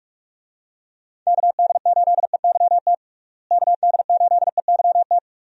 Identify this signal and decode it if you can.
Morse code, heard as a keyed tone, about 42 wpm, 700 Hz. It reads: KD8EYT KD8EYT
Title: USA Call Signs - 6 Characters - Repeated 3x - 45wpm